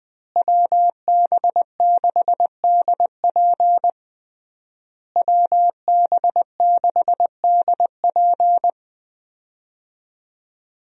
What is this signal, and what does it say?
Morse code, heard as a keyed tone, about 20 wpm, 700 Hz. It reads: WB6DP WB6DP
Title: USA Call Signs - 5 Characters - Repeated 3x - 20wpm